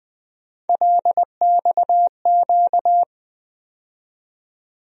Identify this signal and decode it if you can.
Morse code, heard as a keyed tone, about 20 wpm, 700 Hz. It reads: LXQ
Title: USA Call Sign Postfixes - 3 Letters 20wpm